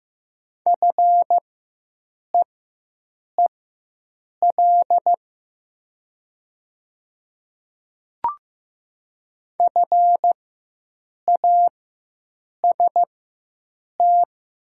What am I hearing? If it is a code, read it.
Morse code, heard as a keyed tone, about 15 wpm, spaced as for 7 wpm, 700 Hz. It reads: FEEL FAST
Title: 4x Character Spacing - Top 500 Words - 15wpm